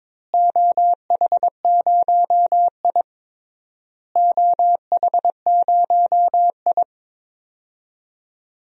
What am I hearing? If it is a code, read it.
Morse code, heard as a keyed tone, about 22 wpm, 700 Hz. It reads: OH0I OH0I
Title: DXCC-100 Call Signs - 4 Characters - Repeated 3x - 22wpm